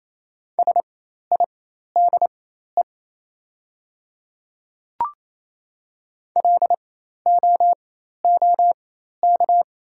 Morse code, heard as a keyed tone, about 28 wpm, spaced as for 13 wpm, 700 Hz. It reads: SIDE LOOK
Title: 4x Character Spacing - Top 100 Words - 28wpm